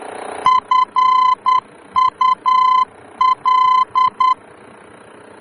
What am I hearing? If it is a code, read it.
Morse code, heard as a keyed tone, about 10 wpm, 1020 Hz. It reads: FUL
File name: germany / mx / FUL-D-VOR